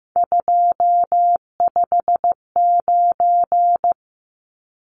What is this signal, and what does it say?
Morse code, heard as a keyed tone, about 15 wpm, 700 Hz. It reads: 259
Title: Numbers - 3-Digit 15wpm